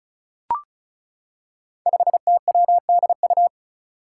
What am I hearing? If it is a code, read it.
Morse code, heard as a keyed tone, about 36 wpm, 700 Hz. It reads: 5TWDU
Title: Five Challenging Letter-Number Combinations 35wpm